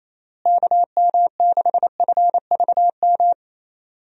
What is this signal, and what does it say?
Morse code, heard as a keyed tone, about 28 wpm, 700 Hz. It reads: KM6FVM
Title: USA Call Signs - 6 Characters - Repeated 2x - 28wpm